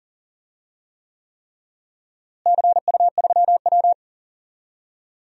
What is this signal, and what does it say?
Morse code, heard as a keyed tone, about 40 wpm, 700 Hz. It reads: CU3W
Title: DXCC-100 Call Signs - 4 Characters - Repeated 3x - 40wpm